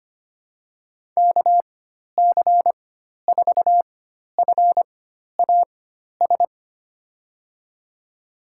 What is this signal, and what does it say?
Morse code, heard as a keyed tone, about 25 wpm, spaced as for 12 wpm, 700 Hz. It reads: KC4FAS
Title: USA Call Signs - 4x Character Spacing - 25wpm